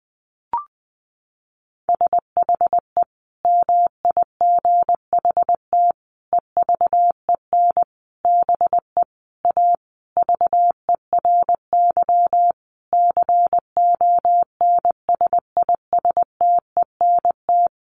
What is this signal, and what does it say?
Morse code, heard as a keyed tone, about 20 wpm, 700 Hz. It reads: SHE MIGHT EVEN BE A VERY CONSISTENT
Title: Sentences from Top 1500 Words 20wpm